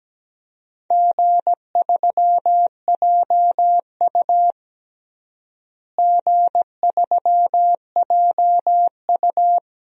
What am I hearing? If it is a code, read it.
Morse code, heard as a keyed tone, about 17 wpm, 700 Hz. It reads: G3JU G3JU
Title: DXCC-100 Call Signs - 4 Characters - Repeated 2x - 17wpm